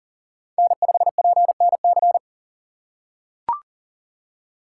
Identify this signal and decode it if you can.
Morse code, heard as a keyed tone, about 40 wpm, 700 Hz. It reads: N5PNC
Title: USA Call Signs - 5 Characters - 6x Character Spacing - 40wpm